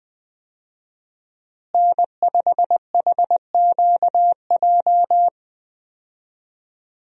Morse code, heard as a keyed tone, about 20 wpm, 700 Hz. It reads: N5HQJ
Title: USA Call Signs - 5 Characters - Repeated 3x - 20wpm